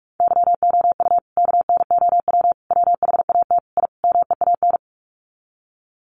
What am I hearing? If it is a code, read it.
Morse code, heard as a keyed tone, about 44 wpm, 700 Hz. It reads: YOU KNOW WHAT I MEAN
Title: Sets of 5 Words 45wpm